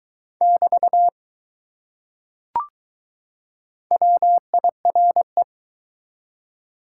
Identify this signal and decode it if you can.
Morse code, heard as a keyed tone, about 20 wpm, 700 Hz. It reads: <BT> WIRE